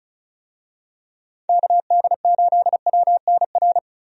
Morse code, heard as a keyed tone, about 36 wpm, 700 Hz. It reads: KD8WNR
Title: USA Call Signs - 6 Characters - 35wpm